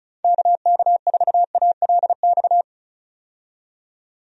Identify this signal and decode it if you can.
Morse code, heard as a keyed tone, about 34 wpm, 700 Hz. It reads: KK4ALX
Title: USA Call Signs - 6 Characters - 35wpm